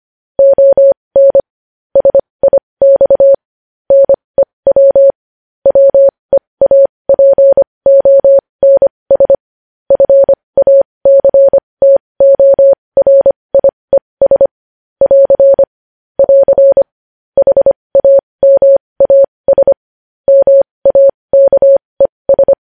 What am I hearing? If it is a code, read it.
Morse code, heard as a keyed tone, about 25 wpm, spaced as for 20 wpm, 550 Hz. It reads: ON SIX NEW WEAPONS FACTORIES <AR> <AR> HAMAS MAKES
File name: podcast / News headlines-1x-25x20